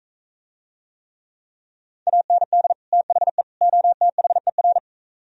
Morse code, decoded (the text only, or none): AND THE OTHER